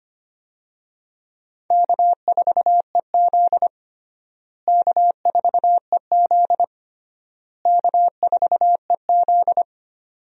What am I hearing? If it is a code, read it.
Morse code, heard as a keyed tone, about 25 wpm, 700 Hz. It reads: K4EZ K4EZ K4EZ
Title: USA Call Signs - 4 Characters - Repeated 3x - 25wpm